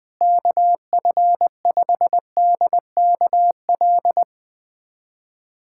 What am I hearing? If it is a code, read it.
Morse code, heard as a keyed tone, about 21 wpm, 700 Hz. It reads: KF5DKL
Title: USA Callsign DE Callsign - 6 Characters - 20wpm